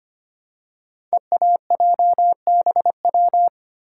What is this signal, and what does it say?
Morse code, heard as a keyed tone, about 25 wpm, 700 Hz. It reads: EAJBW
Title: Code Groups - 5 Letters - 25wpm